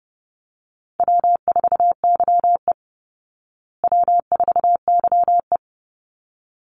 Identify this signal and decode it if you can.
Morse code, heard as a keyed tone, about 30 wpm, 700 Hz. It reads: W4YE W4YE
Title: Easy Call Signs Repeated 2x - 30wpm